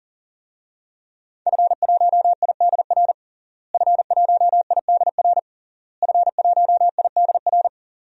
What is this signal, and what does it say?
Morse code, heard as a keyed tone, about 40 wpm, 700 Hz. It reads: F1IDR F1IDR F1IDR
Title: DXCC-100 Call Signs - 5 Characters - Repeated 3x - 40wpm